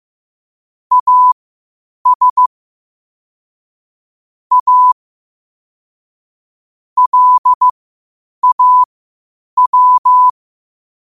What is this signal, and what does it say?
Morse code, heard as a keyed tone, about 15 wpm, spaced as for 8 wpm, 1000 Hz. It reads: AS A LAW
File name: qotdfiles / 8wpm